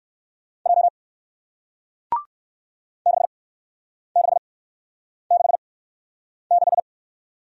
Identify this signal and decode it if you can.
Morse code, heard as a keyed tone, about 58 wpm, spaced as for 9 wpm, 700 Hz. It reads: J 6666